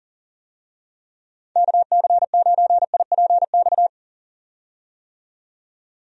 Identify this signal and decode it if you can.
Morse code, heard as a keyed tone, about 40 wpm, 700 Hz. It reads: KC9IPX
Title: USA Call Signs - 6 Characters - 40wpm